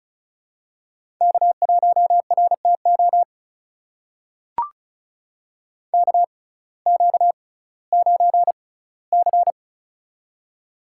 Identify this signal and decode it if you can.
Morse code, heard as a keyed tone, about 36 wpm, spaced as for 19 wpm, 700 Hz. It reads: K1RTO KQ9C
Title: USA Call Signs - 6x Character Spacing - 35wpm